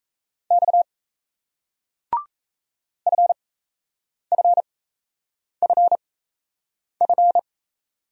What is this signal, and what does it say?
Morse code, heard as a keyed tone, about 36 wpm, spaced as for 8 wpm, 700 Hz. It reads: X FFFF